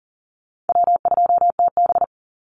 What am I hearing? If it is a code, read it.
Morse code, heard as a keyed tone, about 40 wpm, 700 Hz. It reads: W2TB
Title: Call Signs Repeated 2x - 40wpm